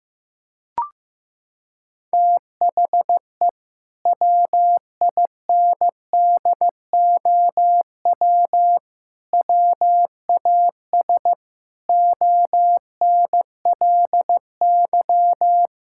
Morse code, heard as a keyed tone, about 16 wpm, 700 Hz. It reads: THE WINDOW WAS ONLY
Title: Sentences from Top 1600 Words 15wpm